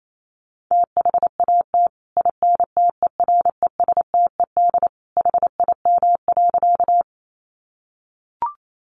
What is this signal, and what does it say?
Morse code, heard as a keyed tone, about 28 wpm, 700 Hz. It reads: THAT INTERESTED HIM.
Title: Sentences from Top 900 Words 28wpm